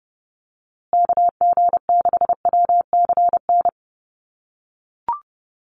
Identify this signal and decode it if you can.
Morse code, heard as a keyed tone, about 29 wpm, 700 Hz. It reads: KG6WCN
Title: USA Call Signs Repeated 3x 30wpm